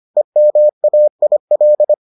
Morse code, heard as a keyed tone, about 25 wpm, 600 Hz. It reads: EMAIL